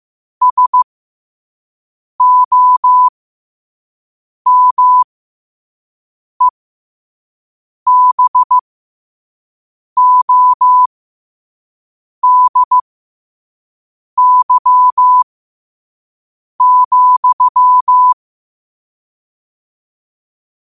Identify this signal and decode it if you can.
Morse code, heard as a keyed tone, about 15 wpm, spaced as for 5 wpm, 1000 Hz. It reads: SOMEBODY,
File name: qotdfiles / 5wpm